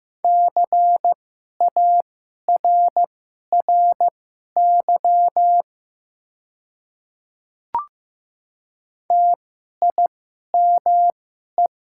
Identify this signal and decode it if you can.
Morse code, heard as a keyed tone, about 15 wpm, spaced as for 11 wpm, 700 Hz. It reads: CARRY TIME
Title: 2x Character Spacing - Top 500 Words - 15wpm